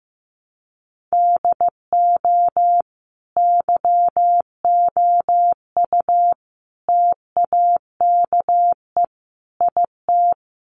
Morse code, heard as a keyed tone, about 15 wpm, 700 Hz. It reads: DO YOU TAKE IT
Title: Sentences from Top 1200 Words 15wpm